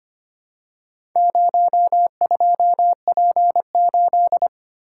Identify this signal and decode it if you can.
Morse code, heard as a keyed tone, about 25 wpm, 700 Hz. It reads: 02P8
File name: practice / 25wpm / Four Letter-Number Combinations - 25wpm